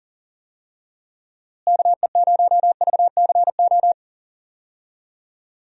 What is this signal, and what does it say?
Morse code, heard as a keyed tone, about 40 wpm, 700 Hz. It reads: KE0VCO
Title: USA Call Signs - 6 Characters - Repeated 2x - 40wpm